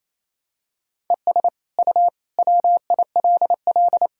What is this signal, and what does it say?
Morse code, heard as a keyed tone, about 28 wpm, 700 Hz. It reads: ES U WILL
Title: Sentences from Top 400 Words and Abbreviations 28wpm